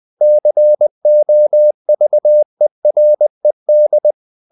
Morse code, heard as a keyed tone, about 20 wpm, 600 Hz. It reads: COVERED
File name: the old man and the sea / the old man and the sea part 1 20wpm